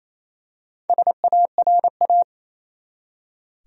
Morse code, heard as a keyed tone, about 28 wpm, 700 Hz. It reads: SARA
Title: USA Common Names - 28wpm